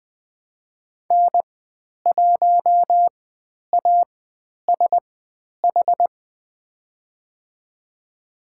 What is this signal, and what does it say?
Morse code, heard as a keyed tone, about 20 wpm, spaced as for 10 wpm, 700 Hz. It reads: N1ASH
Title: 102 Lesson 17g - Letter u - Call Signs 20wpm